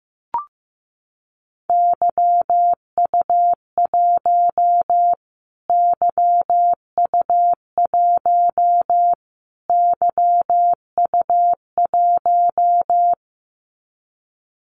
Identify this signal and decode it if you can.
Morse code, heard as a keyed tone, about 15 wpm, 700 Hz. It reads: YU1 YU1 YU1
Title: Callsign Prefixes for Top 25 DXCC - Repeated 3x - 15wpm